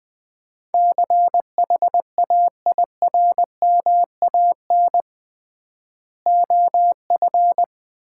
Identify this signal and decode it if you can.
Morse code, heard as a keyed tone, about 20 wpm, 700 Hz. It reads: CHAIRMAN OF
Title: 3x Word-Spacing - Sets of 3 Words - 20wpm